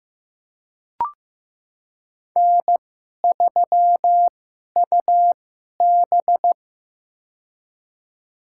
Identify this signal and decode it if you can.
Morse code, heard as a keyed tone, about 15 wpm, spaced as for 11 wpm, 700 Hz. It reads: N3UB
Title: USA Call Signs - 4 Characters - 2x Character Spacing - 15wpm